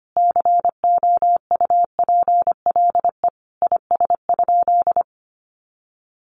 Morse code, heard as a keyed tone, about 25 wpm, 700 Hz. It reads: COUPLE IS?
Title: Sentences from Top 1100 Words 25wpm